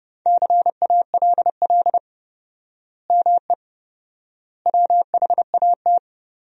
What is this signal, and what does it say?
Morse code, heard as a keyed tone, about 30 wpm, 700 Hz. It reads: CALL ME WHAT
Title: 4x Word-Spacing - Sentences from Top 200 Words - 30wpm